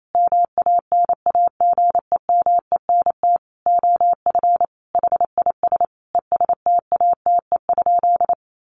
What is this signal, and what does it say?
Morse code, heard as a keyed tone, about 28 wpm, 700 Hz. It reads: MANAGEMENT OF HIS ESTATE?
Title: Sentences from Top 1200 Words 28wpm